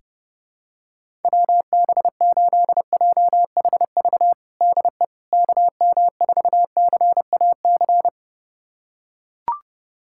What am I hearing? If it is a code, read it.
Morse code, heard as a keyed tone, about 32 wpm, 700 Hz. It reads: WB8JHV DE KM4CAC
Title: USA Callsign DE Callsign 30wpm